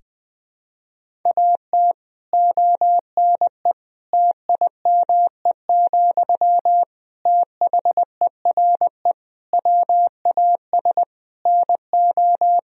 Morse code, heard as a keyed tone, about 20 wpm, 700 Hz. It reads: AT ONE TIME, THERE WAS NO
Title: Sentences from Top 1000 Words - Review Part 1 - 20wpm